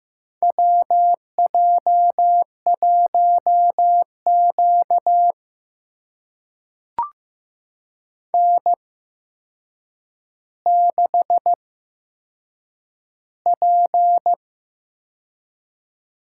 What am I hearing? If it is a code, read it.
Morse code, heard as a keyed tone, about 15 wpm, spaced as for 7 wpm, 700 Hz. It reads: WJ1Q N6P
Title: USA Call Signs - 4 Characters - 8x Character Spacing - 15wpm